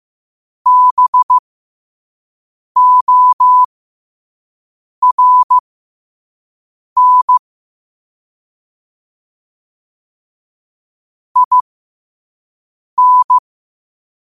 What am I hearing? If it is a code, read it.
Morse code, heard as a keyed tone, about 15 wpm, spaced as for 5 wpm, 1000 Hz. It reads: BORN IN
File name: qotdfiles / 5wpm